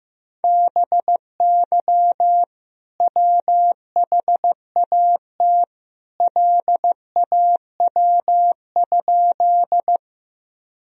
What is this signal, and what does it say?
Morse code, heard as a keyed tone, about 15 wpm, 700 Hz. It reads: BY WHAT LAW?
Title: Sentences from Top 1000 Words - Review Part 3 - 15wpm